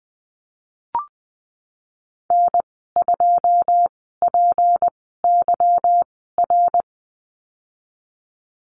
Morse code, heard as a keyed tone, about 19 wpm, spaced as for 14 wpm, 700 Hz. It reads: N2PYR
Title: 179 Lesson 29g - Slash - Call Signs 20wpm